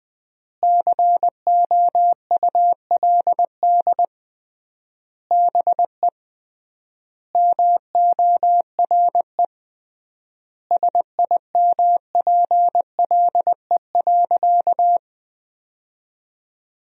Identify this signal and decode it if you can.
Morse code, heard as a keyed tone, about 20 wpm, 700 Hz. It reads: COULD BE MORE SIMPLE.
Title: 3x Word-Spacing - Sentences from Top 500 Words - 20wpm